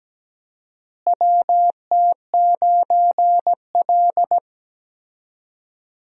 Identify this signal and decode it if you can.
Morse code, heard as a keyed tone, about 17 wpm, 700 Hz. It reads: WT9L
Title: USA Call Signs - 4 Characters - Repeated 3x - 17wpm